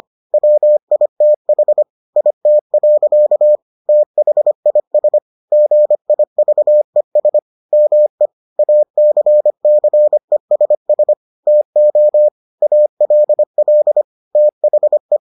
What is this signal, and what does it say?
Morse code, heard as a keyed tone, about 25 wpm, 600 Hz. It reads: WITH IT. THIS GIVES ME ACCESS TO ALL THE